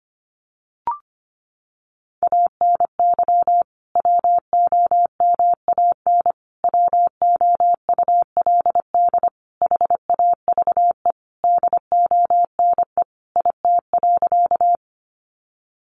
Morse code, heard as a keyed tone, about 25 wpm, 700 Hz. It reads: ANY WOMAN WOULD HAVE DONE IT.
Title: Sentences from Top 1100 Words 25wpm